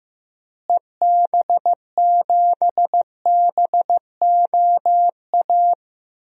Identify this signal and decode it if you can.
Morse code, heard as a keyed tone, about 15 wpm, 700 Hz. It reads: EB7BOA
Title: DXCC-100 Call Signs - 6 Characters - Repeated 3x - 15wpm